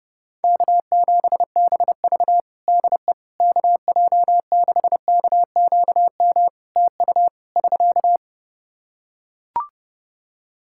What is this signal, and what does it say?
Morse code, heard as a keyed tone, about 31 wpm, 700 Hz. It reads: K7BV DE KJ6KQM TU <SK>
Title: Practice QSO - Exchange 4 - Ending - 30wpm